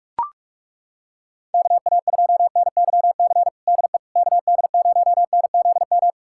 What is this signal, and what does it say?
Morse code, heard as a keyed tone, about 41 wpm, 700 Hz. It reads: KA2NYC DE KD0NZM
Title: USA Callsign DE Callsign - 6 Characters - 45wpm